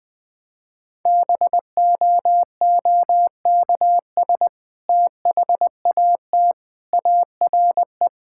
Morse code, heard as a keyed tone, about 21 wpm, 700 Hz. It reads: BOOKS THAT ARE
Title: Sentences with 2-Word Phrases - Spoken Beginning - 20wpm